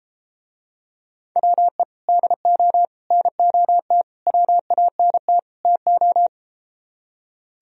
Morse code, heard as a keyed tone, about 33 wpm, 700 Hz. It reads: WE DO NOT WANT TO